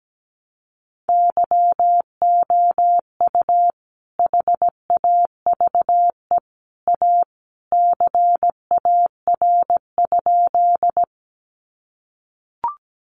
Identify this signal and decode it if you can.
Morse code, heard as a keyed tone, about 17 wpm, 700 Hz. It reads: YOU HAVE A CAR?
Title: Sentences from Top 300 Words 17wpm